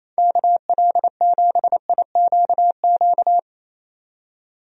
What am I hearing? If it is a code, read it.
Morse code, heard as a keyed tone, about 28 wpm, 700 Hz. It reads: KL7IQQ
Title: DXCC-100 Call Signs - 6 Characters - 28wpm